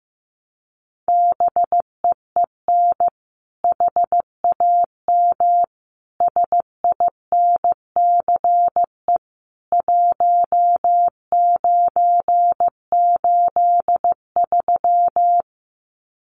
Morse code, heard as a keyed tone, about 15 wpm, 700 Hz. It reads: BEEN HAM SINCE 1983
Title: Common QSO Phrases 15wpm